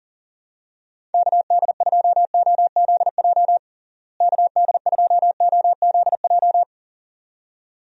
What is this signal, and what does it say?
Morse code, heard as a keyed tone, about 37 wpm, 700 Hz. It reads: KD2OZJ KD2OZJ
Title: USA Call Signs - 6 Characters - Repeated 2x - 40wpm